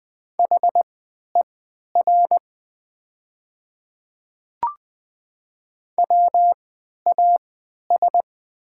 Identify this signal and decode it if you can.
Morse code, heard as a keyed tone, about 20 wpm, spaced as for 11 wpm, 700 Hz. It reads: HER WAS
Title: 3x Character Spacing - Top 100 Words - 20wpm